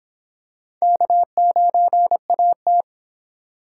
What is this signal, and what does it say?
Morse code, heard as a keyed tone, about 25 wpm, 700 Hz. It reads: K9AT